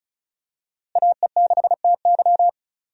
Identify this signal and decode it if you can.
Morse code, heard as a keyed tone, about 34 wpm, 700 Hz. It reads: AE6TY
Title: USA Call Signs - 5 Characters - Repeated 2x - 35wpm